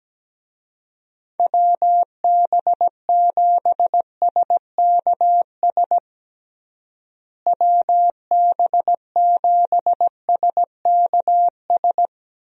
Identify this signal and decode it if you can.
Morse code, heard as a keyed tone, about 17 wpm, 700 Hz. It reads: WB7SKS WB7SKS
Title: USA Call Signs - 6 Characters - Repeated 3x - 17wpm